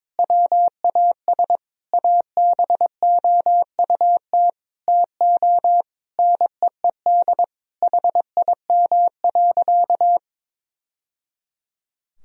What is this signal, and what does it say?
Morse code, heard as a keyed tone, about 22 wpm, 700 Hz. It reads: WAS ABOUT TO NEED HIM.
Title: Sentences from Top 1000 Words - Review Part 1 - 22wpm